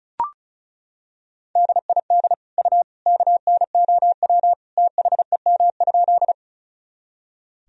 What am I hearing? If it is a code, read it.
Morse code, heard as a keyed tone, about 34 wpm, 700 Hz. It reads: DID U KNOW THEM?
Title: Sentences from Top 100 Words and Abbreviations 35wpm